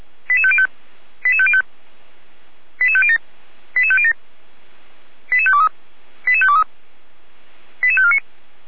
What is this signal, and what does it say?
Five-tone selective call (ZVEI), heard as quick runs of five tones, tones of about 70 ms. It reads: 88505 88505 88507 88507 88523 88523 88549